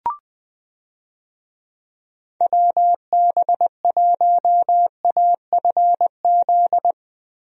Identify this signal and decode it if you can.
Morse code, heard as a keyed tone, about 21 wpm, 700 Hz. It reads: WB1AFZ
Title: USA Call Signs - 6 Characters - Repeated 3x - 20wpm